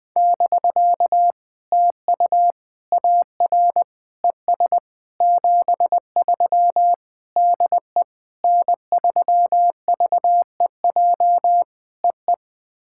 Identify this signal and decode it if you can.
Morse code, heard as a keyed tone, about 21 wpm, 700 Hz. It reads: <BK> TU AR ES 73 DE N3VEJ EE
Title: POTA QSO Practice - 20wpm